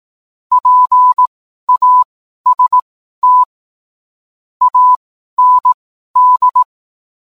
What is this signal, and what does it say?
Morse code, heard as a keyed tone, about 18 wpm, spaced as for 12 wpm, 1000 Hz. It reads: PAST AND